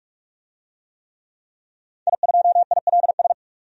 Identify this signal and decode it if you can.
Morse code, heard as a keyed tone, about 43 wpm, 700 Hz. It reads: I2ILS
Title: DXCC-100 Call Signs - 5 Characters - 45wpm